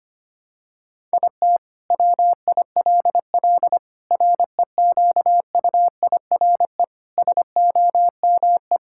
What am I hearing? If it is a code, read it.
Morse code, heard as a keyed tone, about 25 wpm, 700 Hz. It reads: IT WILL REQUIRE SOME
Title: Sentences from Top 1000 Words - Review Part 6 - 25wpm